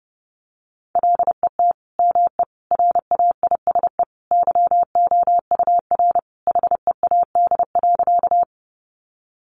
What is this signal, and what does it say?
Morse code, heard as a keyed tone, about 30 wpm, 700 Hz. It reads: LET ME RAISE YOUR HEAD.
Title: Sentences from Top 1400 Words 30wpm